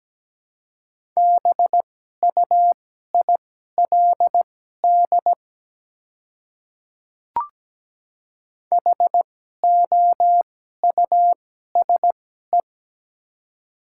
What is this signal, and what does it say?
Morse code, heard as a keyed tone, about 17 wpm, spaced as for 12 wpm, 700 Hz. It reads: BUILD HOUSE
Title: 2x Character Spacing - Top 300 Words - 17wpm